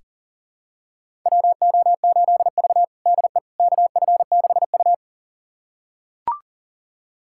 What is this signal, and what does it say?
Morse code, heard as a keyed tone, about 37 wpm, 700 Hz. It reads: WO8V DE KF6U
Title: USA Callsign DE Callsign - 4 Characters - 40wpm